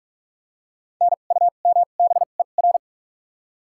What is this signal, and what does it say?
Morse code, heard as a keyed tone, about 46 wpm, spaced as for 32 wpm, 700 Hz. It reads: NUMBER
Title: 2x Character Spacing - Top 500 Words - 45wpm